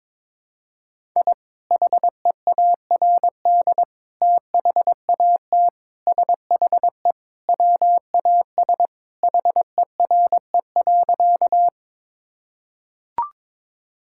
Morse code, heard as a keyed tone, about 22 wpm, 700 Hz. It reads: I HEARD THAT SHE WAS HERE.
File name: practice / 22wpm / Sentences with 3-Word Phrases - 22wpm